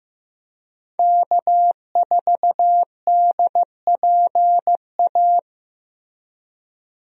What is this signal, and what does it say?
Morse code, heard as a keyed tone, about 15 wpm, 700 Hz. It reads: K4DPA
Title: USA Call Signs - 5 Characters - Repeated 2x - 15wpm